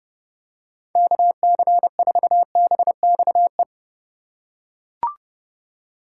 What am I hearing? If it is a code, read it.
Morse code, heard as a keyed tone, about 32 wpm, 700 Hz. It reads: KC4BXE